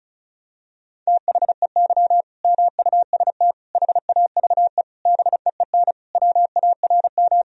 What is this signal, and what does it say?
Morse code, heard as a keyed tone, about 36 wpm, 700 Hz. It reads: THEY MUST HAVE BEEN WARM
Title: Sentences from Top 1000 Words - Review Part 2 - 35wpm